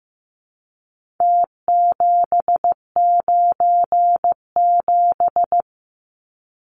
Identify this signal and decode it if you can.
Morse code, heard as a keyed tone, about 15 wpm, 700 Hz. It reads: T797
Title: Numbers - Contest Style - 4-Digit 15wpm